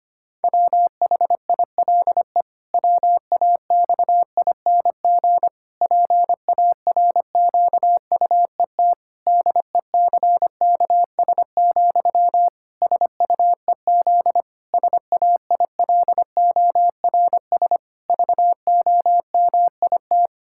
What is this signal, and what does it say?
Morse code, heard as a keyed tone, about 25 wpm, 700 Hz. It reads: WHILE WAXING PARQUET DECKS, SUEZ SAILORS VOMIT
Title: Pangram Challenge v2 - 25wpm